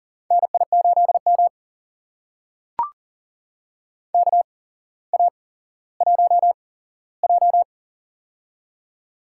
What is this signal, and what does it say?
Morse code, heard as a keyed tone, about 40 wpm, spaced as for 17 wpm, 700 Hz. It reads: NI8M KA1J
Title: USA Call Signs - 8x Character Spacing - 40wpm